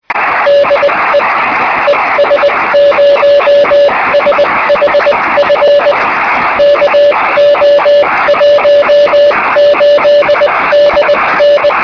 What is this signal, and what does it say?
Morse code, heard as a keyed tone, about 20 wpm, spaced as for 17 wpm, 580 Hz, with noise. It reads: DE ES0SHF KO18DN
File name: sm7lcb / 1837z es0shf-b 23cm